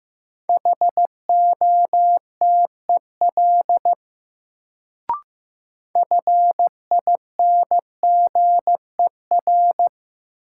Morse code, heard as a keyed tone, about 15 wpm, 700 Hz. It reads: HOTEL FINGER
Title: Sentences from Top 1600 Words - New Word Drill Alt - 15wpm